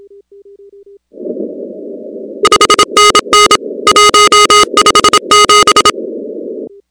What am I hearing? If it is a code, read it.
Morse code, heard as a keyed tone, about 27 wpm, 1250 Hz, with noise. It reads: 5NN 157
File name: contests / 2019-cw-wpx-kt5j-k5pi- -2330 0-16